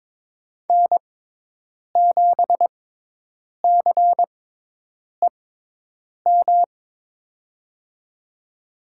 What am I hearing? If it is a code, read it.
Morse code, heard as a keyed tone, about 22 wpm, spaced as for 8 wpm, 700 Hz. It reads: N7CEM
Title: USA Call Signs - 5 Characters - 6x Character Spacing - 22wpm